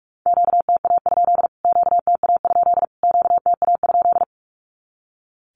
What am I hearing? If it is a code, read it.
Morse code, heard as a keyed tone, about 44 wpm, 700 Hz. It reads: QTA? QTA? QTA?
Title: Q-Codes Repeated 3x 45wpm